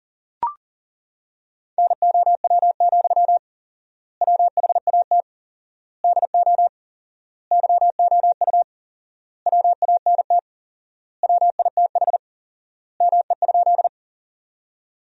Morse code, heard as a keyed tone, about 40 wpm, 700 Hz. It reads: NOW, WHAT DO YOU WANT WITH ME?
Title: 4x Word-Spacing - Sentences from Top 200 Words - 40wpm